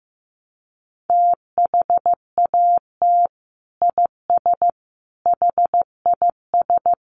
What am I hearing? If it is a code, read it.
Morse code, heard as a keyed tone, about 14 wpm, 700 Hz. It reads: THAT IS HIS
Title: Sentences from Top 900 Words 15wpm